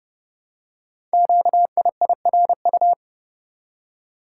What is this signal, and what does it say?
Morse code, heard as a keyed tone, about 30 wpm, 700 Hz. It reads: QIIRU
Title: Code Groups - 5 Letters - 30wpm